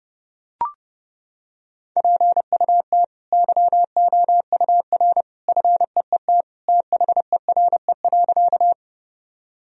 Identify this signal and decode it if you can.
Morse code, heard as a keyed tone, about 32 wpm, 700 Hz. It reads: PUT YOUR FEET THERE.
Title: Sentences from Top 1000 Words - Review Part 1 - 30wpm